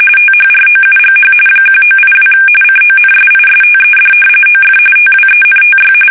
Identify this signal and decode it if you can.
RTTY, about 45.45 baud. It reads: THROUGH 1982 CONSISTED OF TWENTY
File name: fsk850 / battleshipnj